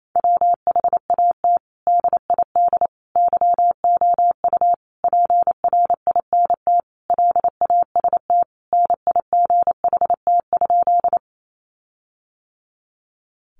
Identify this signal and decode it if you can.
Morse code, heard as a keyed tone, about 28 wpm, 700 Hz. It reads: WHAT DID YOU PRINT LAST NIGHT?
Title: Sentences from Top 1400 Words 28wpm